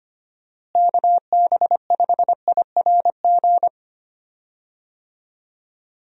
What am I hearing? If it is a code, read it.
Morse code, heard as a keyed tone, about 25 wpm, 700 Hz. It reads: KB5IRG
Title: USA Call Signs - 6 Characters - Repeated 3x - 25wpm